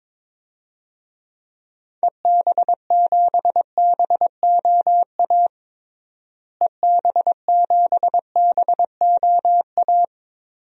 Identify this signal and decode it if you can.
Morse code, heard as a keyed tone, about 21 wpm, 700 Hz. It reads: EB7BOA EB7BOA
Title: DXCC-100 Call Signs - 6 Characters - Repeated 3x - 22wpm